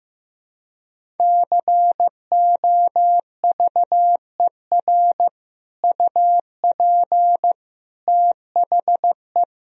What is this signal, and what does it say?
Morse code, heard as a keyed tone, about 15 wpm, 700 Hz. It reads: COVER UP THE
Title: Sentences from Top 1000 Words - Review Part 1 - 15wpm